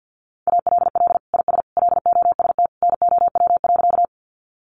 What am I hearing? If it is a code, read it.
Morse code, heard as a keyed tone, about 50 wpm, 700 Hz. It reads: ALL IS LOST NOW.
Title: Sentences from Top 1400 Words 50wpm